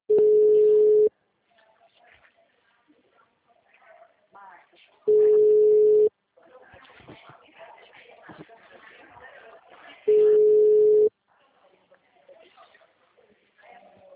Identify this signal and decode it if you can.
Ringback tone (Europe, 425 Hz), with speech.
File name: out- -2003- - - .851